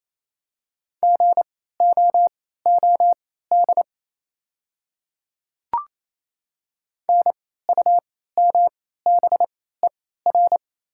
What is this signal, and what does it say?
Morse code, heard as a keyed tone, about 28 wpm, spaced as for 16 wpm, 700 Hz. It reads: GOOD NUMBER
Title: 3x Character Spacing - Top 200 Words - 28wpm